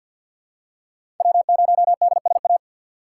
Morse code, heard as a keyed tone, about 52 wpm, 700 Hz. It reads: W0DSA
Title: USA Call Signs - 3x Character Spacing - 50wpm